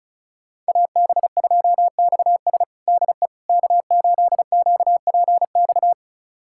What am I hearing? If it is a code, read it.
Morse code, heard as a keyed tone, about 34 wpm, 700 Hz. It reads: AB2XS DE K8QPX
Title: USA Callsign DE Callsign - 5 Characters - 35wpm